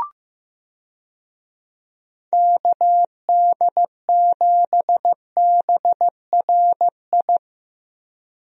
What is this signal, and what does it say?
Morse code, heard as a keyed tone, about 15 wpm, 700 Hz. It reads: KD7BRI
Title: USA Call Signs - 6 Characters - Repeated 3x - 15wpm